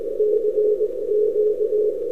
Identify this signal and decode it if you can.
Morse code, heard as a keyed tone, about 20 wpm, spaced as for 17 wpm, 420 Hz. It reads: CQ
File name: n6tr / ru3dx 11-Oct-2008